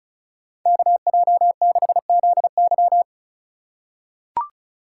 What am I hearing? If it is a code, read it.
Morse code, heard as a keyed tone, about 35 wpm, 700 Hz. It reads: KJ6ZY
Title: USA Call Signs - 5 Characters - 2x Character Spacing - 35wpm